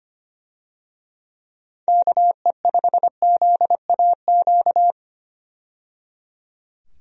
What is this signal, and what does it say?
Morse code, heard as a keyed tone, about 25 wpm, 700 Hz. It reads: KE5ZAQ
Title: USA Call Signs - 6 Characters - 25wpm